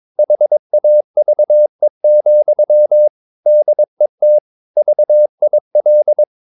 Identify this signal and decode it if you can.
Morse code, heard as a keyed tone, about 22 wpm, 600 Hz. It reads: HAVE, DET VIL
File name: Ferie QRS 22wpm